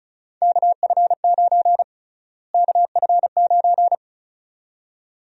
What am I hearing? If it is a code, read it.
Morse code, heard as a keyed tone, about 36 wpm, 700 Hz. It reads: KF9 KF9
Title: USA Call Sign Prefixes - Repeated 2x - 35wpm